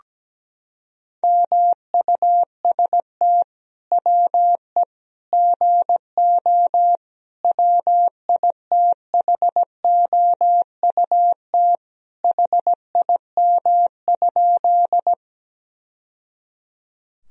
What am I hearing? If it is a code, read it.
Morse code, heard as a keyed tone, about 17 wpm, 700 Hz. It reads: MUST WE GO WITHOUT HIM?
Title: Sentences from Top 1000 Words - Review Part 3 - 17wpm